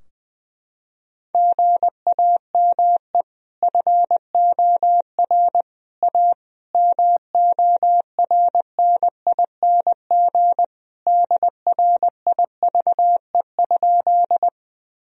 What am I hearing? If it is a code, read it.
Morse code, heard as a keyed tone, about 20 wpm, 700 Hz. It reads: GAME FOR A MORNING DRIVE?
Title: Sentences from Top 1000 Words - Review Part 3 - 20wpm